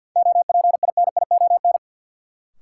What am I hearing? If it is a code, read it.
Morse code, heard as a keyed tone, about 53 wpm, 700 Hz. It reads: OPINION
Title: Sentences from Top 1200 Words - New Word Drill - 50wpm